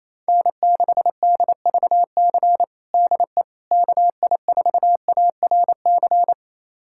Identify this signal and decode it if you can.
Morse code, heard as a keyed tone, about 28 wpm, 700 Hz. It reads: N6DVC DE KI4ARC
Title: USA Callsign DE Callsign 28wpm